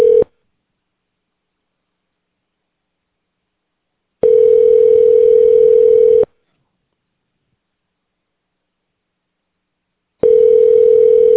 Ringback tone (North America), 440 + 480 Hz.